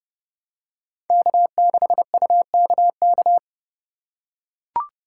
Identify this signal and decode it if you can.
Morse code, heard as a keyed tone, about 30 wpm, 700 Hz. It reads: K6UKK
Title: USA Call Signs - 5 Characters - 4x Character Spacing - 30wpm